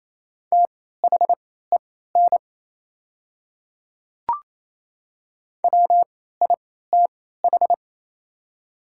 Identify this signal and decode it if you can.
Morse code, heard as a keyed tone, about 28 wpm, spaced as for 16 wpm, 700 Hz. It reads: THEN WITH
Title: 3x Character Spacing - Top 100 Words - 28wpm